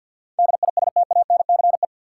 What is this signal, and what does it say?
Morse code, heard as a keyed tone, about 53 wpm, 700 Hz. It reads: DISTANCE
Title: Sentences from Top 1200 Words - New Word Drill - 50wpm